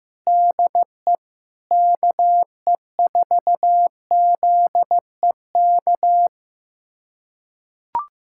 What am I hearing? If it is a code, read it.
Morse code, heard as a keyed tone, about 15 wpm, 700 Hz. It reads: DE KE4ZEK
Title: USA Callsign DE Callsign - 6 Characters - 15wpm